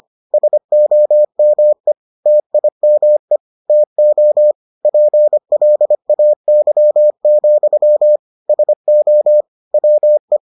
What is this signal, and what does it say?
Morse code, heard as a keyed tone, about 25 wpm, 600 Hz. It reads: SOME TIME TO PLAY, SO WE